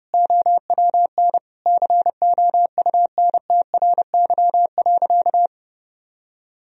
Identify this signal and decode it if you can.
Morse code, heard as a keyed tone, about 30 wpm, 700 Hz. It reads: OWN COUNTRY.
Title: Sentences from Top 1000 Words - Review Part 1 - 30wpm